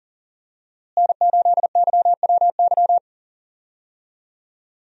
Morse code, heard as a keyed tone, about 40 wpm, 700 Hz. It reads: N8YWY
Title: USA Call Signs - 5 Characters - Repeated 3x - 40wpm